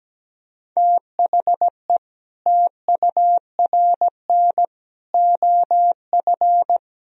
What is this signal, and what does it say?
Morse code, heard as a keyed tone, about 17 wpm, 700 Hz. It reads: THE TURN OF
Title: Speed-Racing - Sets of 5 Words 17wpm